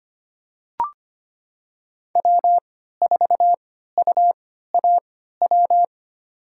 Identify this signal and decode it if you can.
Morse code, heard as a keyed tone, about 25 wpm, spaced as for 14 wpm, 700 Hz. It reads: W4UAW
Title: USA Call Signs - 3x Character Spacing - 25wpm